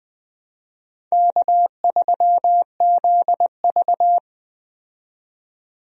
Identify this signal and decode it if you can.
Morse code, heard as a keyed tone, about 20 wpm, 700 Hz. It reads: K3ZV
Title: USA Call Signs - 4 Characters - 20wpm